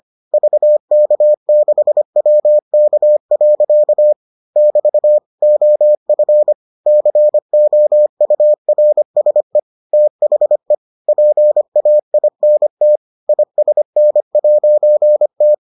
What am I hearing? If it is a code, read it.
Morse code, heard as a keyed tone, about 25 wpm, 600 Hz. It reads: VK6WK. <BT> OF COURSE THE PAINT ISN'T